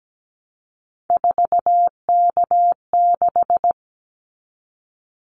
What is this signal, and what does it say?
Morse code, heard as a keyed tone, about 17 wpm, 700 Hz. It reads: 4K6
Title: Callsign Prefixes for Top 100 DXCC - 17wpm